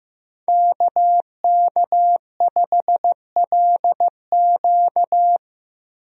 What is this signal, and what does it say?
Morse code, heard as a keyed tone, about 15 wpm, 700 Hz. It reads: KK5LQ
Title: USA Call Signs - 5 Characters - Repeated 2x - 15wpm